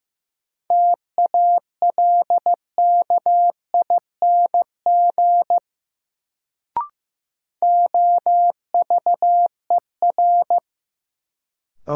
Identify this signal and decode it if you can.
Morse code, heard as a keyed tone, about 15 wpm, 700 Hz. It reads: TALKING OVER
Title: Sets of 2 Words - New Word Drill - 15wpm